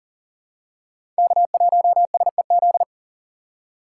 Morse code, heard as a keyed tone, about 40 wpm, 700 Hz. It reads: K1SEZ
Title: USA Call Signs - 5 Characters - 6x Character Spacing - 40wpm